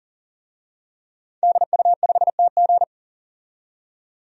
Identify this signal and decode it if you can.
Morse code, heard as a keyed tone, about 40 wpm, 700 Hz. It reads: DU5TG
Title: DXCC-100 Call Signs - 5 Characters - 40wpm